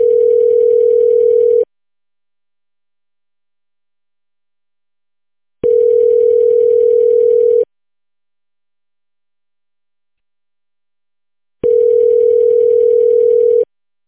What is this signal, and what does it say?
Ringback tone (North America), 440 + 480 Hz.